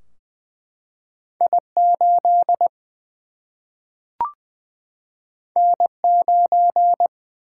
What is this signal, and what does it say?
Morse code, heard as a keyed tone, about 20 wpm, 700 Hz. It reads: I8 N9
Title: Single-Letter Single-Number 20wpm